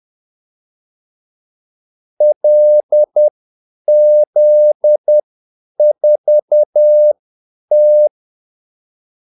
Morse code, heard as a keyed tone, about 10 wpm, spaced as for 8 wpm, 600 Hz. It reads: LZ4T
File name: Callsigns-10-8wpm